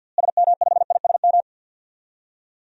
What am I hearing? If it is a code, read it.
Morse code, heard as a keyed tone, about 46 wpm, 700 Hz. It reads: SM5ISM
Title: DXCC-100 Call Signs - 6 Characters - Repeated 3x - 50wpm